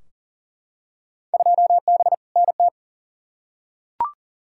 Morse code, heard as a keyed tone, about 39 wpm, 700 Hz. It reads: 2B NT
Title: Field Day Exchanges Encore - 40wpm